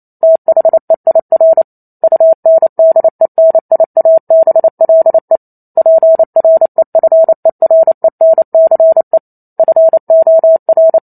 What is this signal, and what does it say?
Morse code, heard as a keyed tone, about 28 wpm, 650 Hz. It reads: THEIR UNDENIABLE PREFERENCE FOR